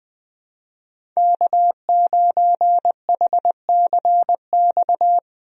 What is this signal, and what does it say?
Morse code, heard as a keyed tone, about 20 wpm, 700 Hz. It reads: K9HCX
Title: USA Call Signs - 5 Characters - 20wpm